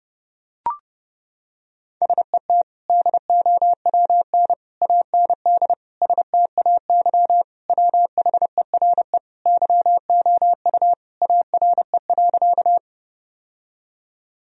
Morse code, heard as a keyed tone, about 32 wpm, 700 Hz. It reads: SET DOWN AND STAY WHERE YOU ARE.
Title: Sentences from Top 1000 Words - Review Part 1 - 30wpm